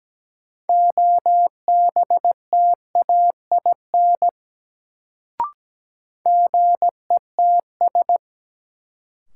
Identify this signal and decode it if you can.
Morse code, heard as a keyed tone, about 17 wpm, 700 Hz. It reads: OBTAIN GETS
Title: Sentences from Top 1300 Words - New Word Drill - 17wpm